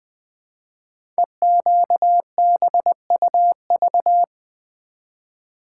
Morse code, heard as a keyed tone, about 20 wpm, 700 Hz. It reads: EQBUV
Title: Code Groups - 5 Letters - 20wpm